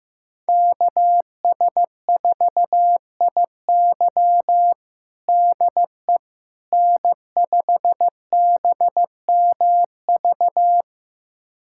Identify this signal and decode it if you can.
Morse code, heard as a keyed tone, about 15 wpm, 700 Hz. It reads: KS4IY DE N5BMV
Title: USA Callsign DE Callsign - 5 Characters - 15wpm